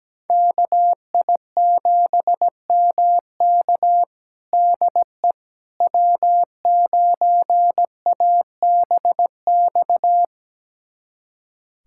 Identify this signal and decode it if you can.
Morse code, heard as a keyed tone, about 17 wpm, 700 Hz. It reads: KI7MK DE W9ABX
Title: USA Callsign DE Callsign - 5 Characters - 17wpm